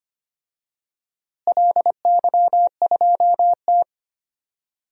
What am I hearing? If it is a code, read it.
Morse code, heard as a keyed tone, about 25 wpm, 700 Hz. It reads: LY2T
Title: DXCC-100 Call Signs - 4 Characters - Repeated 2x - 25wpm